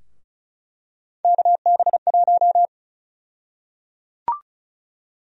Morse code, heard as a keyed tone, about 36 wpm, 700 Hz. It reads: KB1
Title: USA Call Sign Prefixes - 35wpm